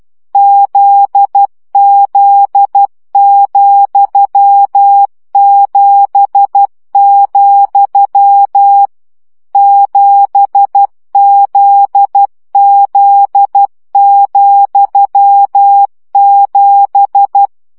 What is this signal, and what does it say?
Morse code, heard as a keyed tone, about 12 wpm, 800 Hz. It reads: ZZ,7, 7ZZ,7
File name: CW gyakorlas z 7 vesszo 10 baud